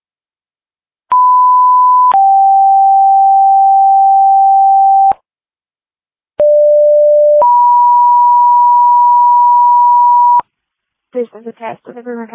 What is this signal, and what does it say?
Two-tone page, heard as a one-second tone, then a three-second tone, about 1000 then 780 Hz, about 590 then 970 Hz, with speech. It is heard from 1.1 to 10.4 s.